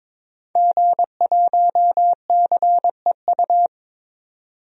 Morse code, heard as a keyed tone, about 22 wpm, 700 Hz. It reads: G1CEU
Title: DXCC-100 Call Signs - 5 Characters - 22wpm